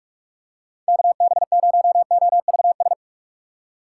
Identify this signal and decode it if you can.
Morse code, heard as a keyed tone, about 46 wpm, 700 Hz. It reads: KB0OVS